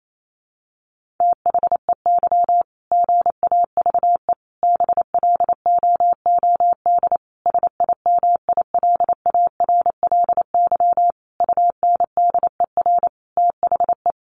Morse code, heard as a keyed tone, about 28 wpm, 700 Hz. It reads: THEY GAVE BLOOD SIMILARLY UNDER THE
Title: Sentences from Top 1400 Words 28wpm